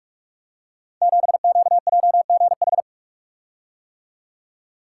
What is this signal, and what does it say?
Morse code, heard as a keyed tone, about 44 wpm, 700 Hz. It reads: 7QJGH
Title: Code Groups - 5 Letter:Numbers - 45wpm